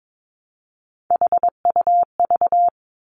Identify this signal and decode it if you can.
Morse code, heard as a keyed tone, about 22 wpm, 700 Hz. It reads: HUV
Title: Three Challenging Letter-Number Combinations 22wpm